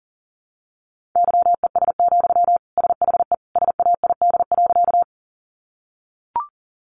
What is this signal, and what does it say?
Morse code, heard as a keyed tone, about 40 wpm, 700 Hz. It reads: YES, SHE SAID.
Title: Sentences from Top 500 Words 40wpm